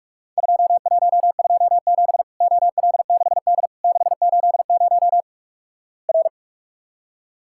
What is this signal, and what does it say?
Morse code, heard as a keyed tone, about 46 wpm, 700 Hz. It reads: 2127 OL6D 680 R
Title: Worked All Europe DX Contest - QTC Exchange - 45wpm